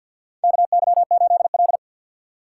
Morse code, heard as a keyed tone, about 47 wpm, 700 Hz. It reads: KY8L
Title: USA Call Signs - 3x Character Spacing - 50wpm